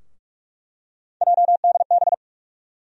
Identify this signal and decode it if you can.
Morse code, heard as a keyed tone, about 42 wpm, 700 Hz. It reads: JDB
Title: USA Call Sign Postfixes - 3 Letters 45wpm